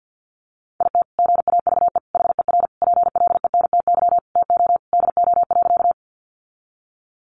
Morse code, heard as a keyed tone, about 50 wpm, 700 Hz. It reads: IT GAVE HER PLENTY TO DO.
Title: Sentences from Top 1300 Words 50wpm